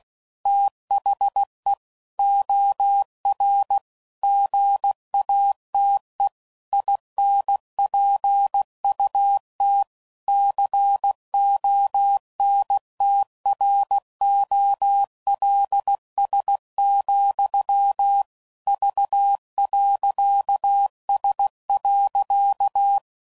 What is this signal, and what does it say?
Morse code, heard as a keyed tone, about 16 wpm, 800 Hz. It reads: THE OR GATE INPUT CONTROLS, V.S.
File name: CW Code Groups @15WPM / Technical Subjects / OR Gate